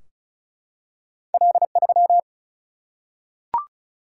Morse code, heard as a keyed tone, about 36 wpm, 700 Hz. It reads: L3
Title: Single-Letter Single-Number 35wpm